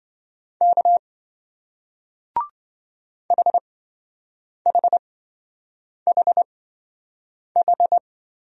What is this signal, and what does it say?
Morse code, heard as a keyed tone, about 26 wpm, spaced as for 7 wpm, 700 Hz. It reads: K HHHH